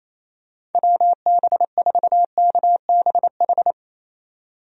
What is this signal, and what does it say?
Morse code, heard as a keyed tone, about 27 wpm, 700 Hz. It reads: WB4KBH
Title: USA Call Signs - 6 Characters - 28wpm